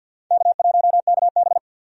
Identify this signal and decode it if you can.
Morse code, heard as a keyed tone, about 50 wpm, 700 Hz. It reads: K1KB
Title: USA Call Signs - 4 Characters - 50wpm